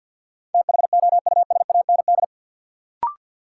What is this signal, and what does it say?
Morse code, heard as a keyed tone, about 53 wpm, 700 Hz. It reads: THOUSAND
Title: Sentences from Top 600 Words - New Word Drill - 50wpm